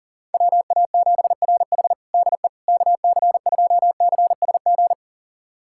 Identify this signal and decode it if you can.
Morse code, heard as a keyed tone, about 37 wpm, 700 Hz. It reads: WA7RH DE KC2CSG
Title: USA Callsign DE Callsign 40wpm